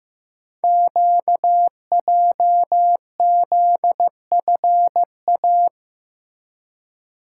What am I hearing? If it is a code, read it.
Morse code, heard as a keyed tone, about 15 wpm, 700 Hz. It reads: QJZFA
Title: Code Groups - 5 Letters - 15wpm